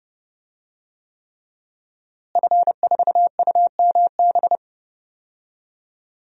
Morse code, heard as a keyed tone, about 30 wpm, 700 Hz. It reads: F4UMB
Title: DXCC-100 Call Signs - 5 Characters - Repeated 3x - 30wpm